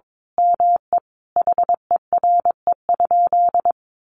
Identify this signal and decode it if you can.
Morse code, heard as a keyed tone, about 22 wpm, 700 Hz. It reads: ME HERE?
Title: Sentences from Top 1200 Words 22wpm